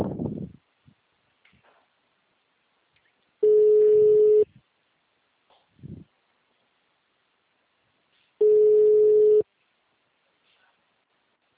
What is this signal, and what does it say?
Ringback tone (Europe, 425 Hz).